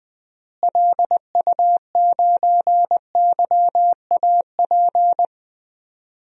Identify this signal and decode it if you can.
Morse code, heard as a keyed tone, about 21 wpm, 700 Hz. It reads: LU9YAP
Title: DXCC-100 Call Signs - 6 Characters - Repeated 3x - 20wpm